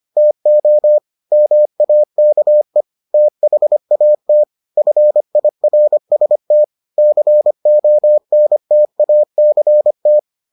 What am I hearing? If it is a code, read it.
Morse code, heard as a keyed tone, about 25 wpm, 600 Hz. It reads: TO MAKE THAT FIRST CONTACT